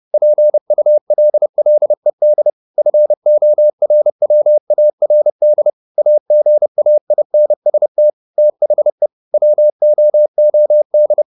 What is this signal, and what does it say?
Morse code, heard as a keyed tone, about 30 wpm, 600 Hz. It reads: PULLED FORWARD AGAINST THE WOOD